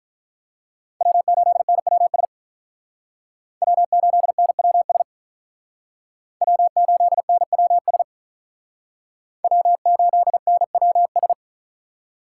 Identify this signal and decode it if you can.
Morse code, heard as a keyed tone, about 45 wpm, 700 Hz. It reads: W8NWS W8NWS W8NWS W8NWS